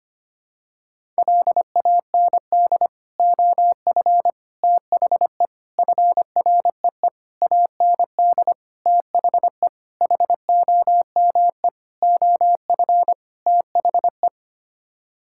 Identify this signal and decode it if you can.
Morse code, heard as a keyed tone, about 25 wpm, 700 Hz. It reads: LAND OF THE FREE AND THE HOME OF THE
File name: Sets of 9 Words - 25wpm